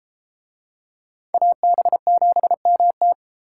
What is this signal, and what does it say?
Morse code, heard as a keyed tone, about 30 wpm, 700 Hz. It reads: AB7MT